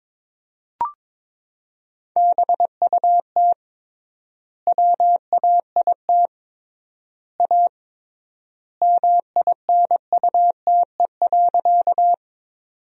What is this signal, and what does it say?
Morse code, heard as a keyed tone, about 22 wpm, 700 Hz. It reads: BUT WAIT A MINUTE.
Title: 3x Word-Spacing - Sentences from Top 500 Words - 22wpm